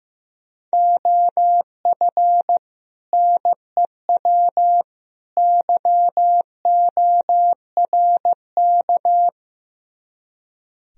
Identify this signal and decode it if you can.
Morse code, heard as a keyed tone, about 15 wpm, 700 Hz. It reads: OF NEW YORK
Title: Speed-Racing - Sets of 3 Words 15wpm